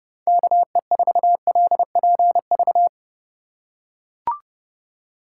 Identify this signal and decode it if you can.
Morse code, heard as a keyed tone, about 31 wpm, 700 Hz. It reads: KE4LPV
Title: USA Call Signs - 3x Character Spacing - 30wpm